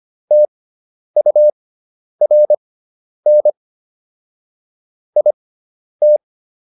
Morse code, heard as a keyed tone, about 25 wpm, spaced as for 10 wpm, 600 Hz. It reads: TURN IT